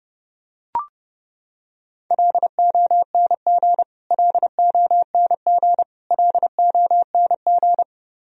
Morse code, heard as a keyed tone, about 31 wpm, 700 Hz. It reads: LONG LONG LONG
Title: Top 100 Words Repeated 30wpm